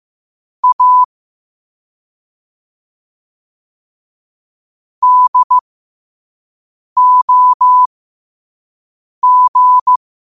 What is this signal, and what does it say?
Morse code, heard as a keyed tone, about 15 wpm, spaced as for 5 wpm, 1000 Hz. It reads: A DOG
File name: qotdfiles / 5wpm